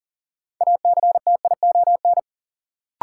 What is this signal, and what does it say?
Morse code, heard as a keyed tone, about 40 wpm, 700 Hz. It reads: ACTION
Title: Sentences from Top 700 Words - New Word Drill - 40wpm